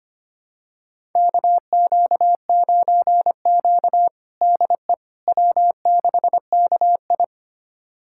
Morse code, heard as a keyed tone, about 25 wpm, 700 Hz. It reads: KQ9Q DE W6KI
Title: USA Callsign DE Callsign - 4 Characters - 25wpm